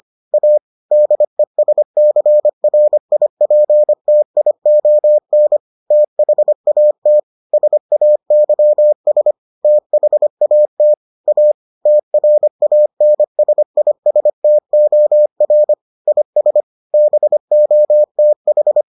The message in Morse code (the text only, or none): A DESCRIPTION THAT SAYS THAT A TRANSISTOR IS BOTH